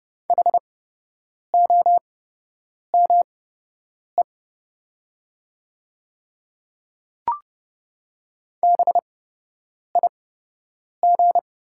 Morse code, heard as a keyed tone, about 30 wpm, spaced as for 8 wpm, 700 Hz. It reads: HOME BIG
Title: 8x Character Spacing - Top 200 Words - 30wpm